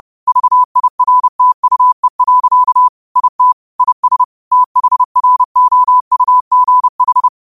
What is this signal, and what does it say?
Morse code, heard as a keyed tone, about 30 wpm, 1000 Hz. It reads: VIRTUE. IT IS THROUGH
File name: qotdfiles / 30wpm